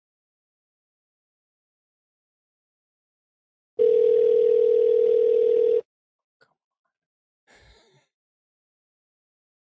Ringback tone (North America), 440 + 480 Hz.